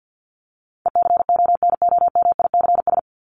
Morse code, heard as a keyed tone, about 50 wpm, 700 Hz. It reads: ECONOMICS
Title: Top 501-750 Words 50wpm